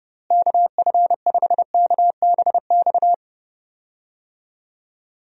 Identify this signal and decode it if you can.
Morse code, heard as a keyed tone, about 32 wpm, 700 Hz. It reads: KF5KBX
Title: USA Call Signs - 6 Characters - Repeated 2x - 30wpm